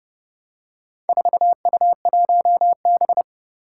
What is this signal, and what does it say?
Morse code, heard as a keyed tone, about 30 wpm, 690 Hz. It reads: 4U1B
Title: Four Challenging Letter-Number Combinations 30wpm